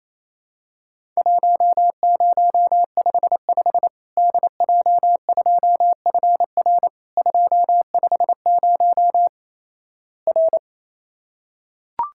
Morse code, heard as a keyed tone, about 28 wpm, 700 Hz. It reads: 1055 DJ2FR 250 R
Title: Worked All Europe DX Contest - QTC Exchange - 28wpm